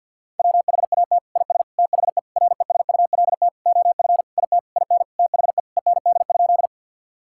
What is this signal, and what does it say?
Morse code, heard as a keyed tone, about 52 wpm, 700 Hz. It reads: WHAT IS THE RESULT OF IT IN THE END?
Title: Sentences from Top 1000 Words - Review Part 5 - 50wpm